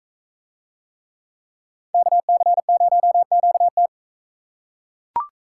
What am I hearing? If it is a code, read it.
Morse code, heard as a keyed tone, about 41 wpm, 700 Hz. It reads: KC0QT